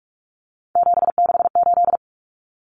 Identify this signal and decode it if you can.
Morse code, heard as a keyed tone, about 44 wpm, 700 Hz. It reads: 768
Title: Common Cut Numbers - 3-Digit 45wpm